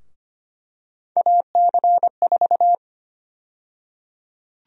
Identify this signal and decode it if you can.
Morse code, heard as a keyed tone, about 25 wpm, 700 Hz. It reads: AC4
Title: USA Call Sign Prefixes - 25wpm